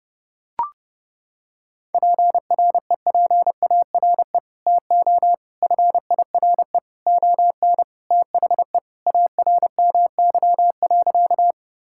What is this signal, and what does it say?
Morse code, heard as a keyed tone, about 32 wpm, 700 Hz. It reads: PREPARE TO FIRE ON THE ARMY.
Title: Sentences from Top 1000 Words - Review Part 6 - 30wpm